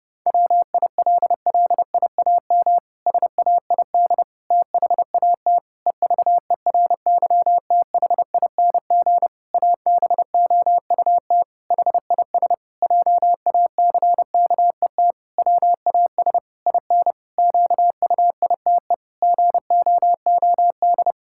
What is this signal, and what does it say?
Morse code, heard as a keyed tone, about 30 wpm, 700 Hz. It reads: WILLIAM SAID THAT EVERYTHING ABOUT HIS JACKET WAS IN QUITE GOOD
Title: Pangram Challenge v2 - 30wpm